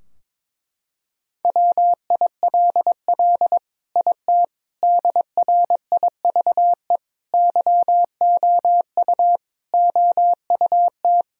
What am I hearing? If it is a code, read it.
Morse code, heard as a keyed tone, about 22 wpm, 700 Hz. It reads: WILL IT DRIVE YOU OUT
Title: Sentences from Top 1000 Words - Review Part 2 - 22wpm